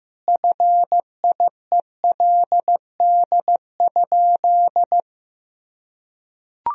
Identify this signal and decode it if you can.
Morse code, heard as a keyed tone, about 15 wpm, 700 Hz. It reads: FIELD?
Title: Sentences from Top 1000 Words - Review Part 2 - 15wpm